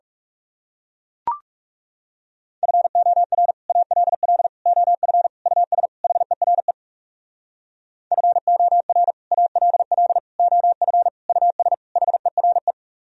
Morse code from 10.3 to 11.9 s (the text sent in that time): OF US